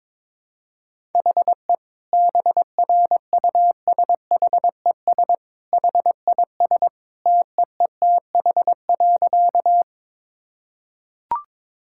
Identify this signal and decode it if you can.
Morse code, heard as a keyed tone, about 22 wpm, 700 Hz. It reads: HE BRUSHES HIS TEETH.
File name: practice / 22wpm / Easy Sentences with Hints - 22wpm